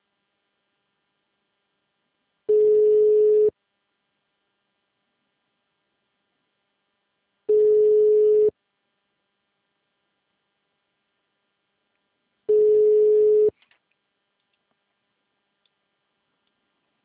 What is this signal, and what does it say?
Ringback tone (Europe, 425 Hz).
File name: out- -2002- - - .525